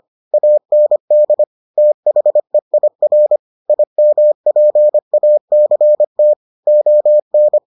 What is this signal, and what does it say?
Morse code, heard as a keyed tone, about 25 wpm, 600 Hz. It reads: AND THEIR IMPACT ON